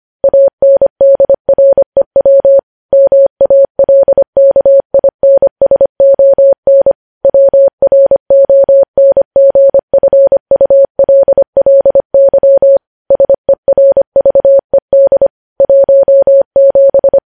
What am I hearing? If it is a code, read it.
Morse code, heard as a keyed tone, about 25 wpm, 550 Hz. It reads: ANDREW MALKINSON WRONGFULLY SERVED 17
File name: podcast / News headlines-1x-25x25